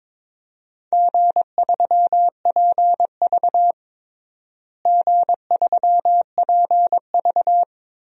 Morse code, heard as a keyed tone, about 22 wpm, 700 Hz. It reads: G3PV G3PV
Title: DXCC-100 Call Signs - 4 Characters - Repeated 3x - 22wpm